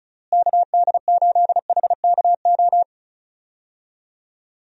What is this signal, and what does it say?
Morse code, heard as a keyed tone, about 34 wpm, 700 Hz. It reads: KD8HKO
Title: USA Call Signs - 6 Characters - Repeated 3x - 35wpm